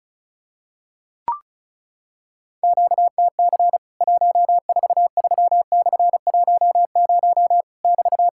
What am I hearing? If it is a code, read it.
Morse code, heard as a keyed tone, about 36 wpm, 700 Hz. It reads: QTC 143/10 <BT>
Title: Worked All Europe DX Contest - QTC Exchange - 35wpm